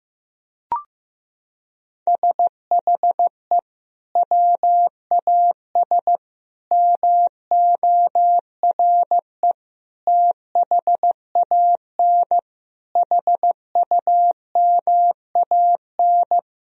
Morse code, heard as a keyed tone, about 16 wpm, 700 Hz. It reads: SHE WAS MORE THAN HUMAN
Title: Sentences from Top 1000 Words - Review Part 3 - 15wpm